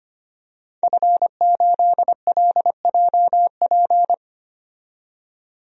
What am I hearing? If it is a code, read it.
Morse code, heard as a keyed tone, about 25 wpm, 700 Hz. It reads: F8LJP
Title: DXCC-100 Call Signs - 5 Characters - 25wpm